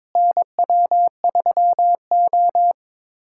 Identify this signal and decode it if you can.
Morse code, heard as a keyed tone, about 21 wpm, 700 Hz. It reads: NW3O
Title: USA Callsign DE Callsign - 4 Characters - 22wpm